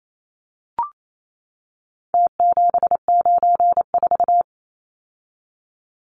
Morse code, heard as a keyed tone, about 28 wpm, 700 Hz. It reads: T794
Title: Numbers - Contest Style - 4-Digit 28wpm